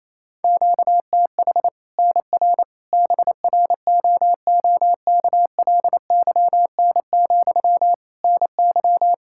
Morse code, heard as a keyed tone, about 28 wpm, 700 Hz. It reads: QTH NR BROOKLYN, NY
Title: Practice QSO - Exchange 1 - Beginning - 28wpm